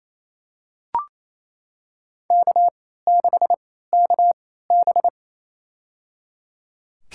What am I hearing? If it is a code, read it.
Morse code, heard as a keyed tone, about 28 wpm, spaced as for 16 wpm, 700 Hz. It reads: K6KB
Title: USA Call Signs - 4 Characters - 3x Character Spacing - 28wpm